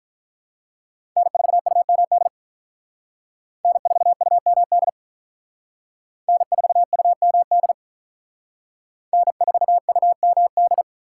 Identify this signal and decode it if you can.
Morse code, heard as a keyed tone, about 44 wpm, 700 Hz. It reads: N4UMD N4UMD N4UMD N4UMD